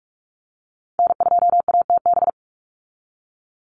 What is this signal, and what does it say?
Morse code, heard as a keyed tone, about 44 wpm, 700 Hz. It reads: N2ATB
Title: Call Signs - 45wpm